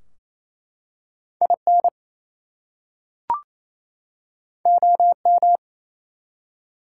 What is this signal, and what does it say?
Morse code, heard as a keyed tone, about 29 wpm, 700 Hz. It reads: IN OM
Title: Top 180 QSO Elements 28wpm